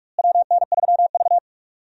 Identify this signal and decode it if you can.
Morse code, heard as a keyed tone, about 46 wpm, 700 Hz. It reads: WN3V
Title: USA Call Signs - 4 Characters - Repeated 3x - 45wpm